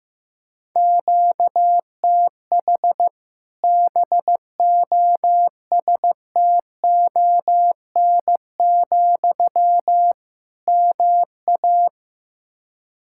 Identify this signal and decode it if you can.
Morse code, heard as a keyed tone, about 15 wpm, 700 Hz. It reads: QTH BOSTON, MA
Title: Practice QSO - Exchange 1 - Beginning - 15wpm